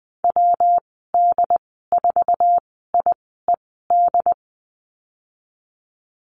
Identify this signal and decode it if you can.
Morse code, heard as a keyed tone, about 19 wpm, spaced as for 14 wpm, 700 Hz. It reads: WD4IED
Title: 173 Lesson 28g - Number 9 - Call Signs 20wpm